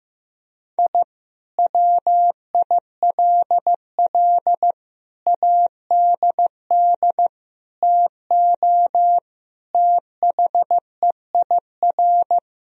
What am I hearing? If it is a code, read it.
Morse code, heard as a keyed tone, about 16 wpm, 700 Hz. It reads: I WILL ADD TO THEIR
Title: Sentences from Top 1000 Words - Review Part 1 - 15wpm